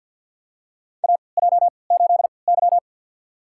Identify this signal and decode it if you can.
Morse code, heard as a keyed tone, about 50 wpm, spaced as for 28 wpm, 700 Hz. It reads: AJ8Y
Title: USA Call Signs - 4 Characters - 3x Character Spacing - 50wpm